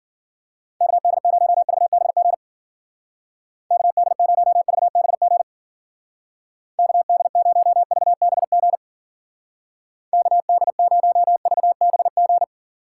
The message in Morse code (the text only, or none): KD0VBG KD0VBG KD0VBG KD0VBG